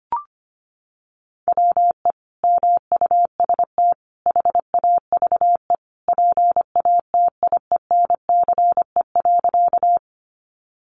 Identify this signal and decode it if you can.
Morse code, heard as a keyed tone, about 25 wpm, 700 Hz. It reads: WE MUST HAVE PATIENCE.
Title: Sentences from Top 1300 Words 25wpm